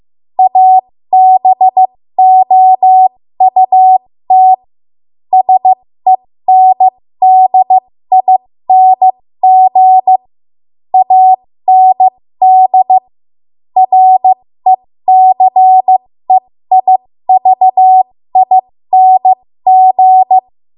Morse code, heard as a keyed tone, about 15 wpm, spaced as for 13 wpm, 750 Hz. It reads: ABOUT SENDING AND RECEIVING
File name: Morse / 13WPM